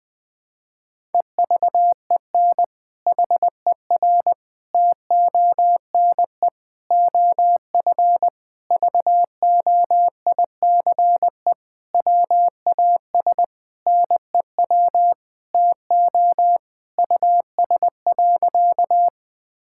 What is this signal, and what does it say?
Morse code, heard as a keyed tone, about 21 wpm, 700 Hz. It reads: EVEN HER TONE OF VOICE WAS NEW TO US.
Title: Sentences from Top 1600 Words 20wpm